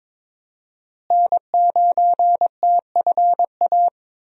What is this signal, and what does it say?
Morse code, heard as a keyed tone, about 22 wpm, 700 Hz. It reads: N9TFA
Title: USA Call Signs - 5 Characters - Repeated 3x - 22wpm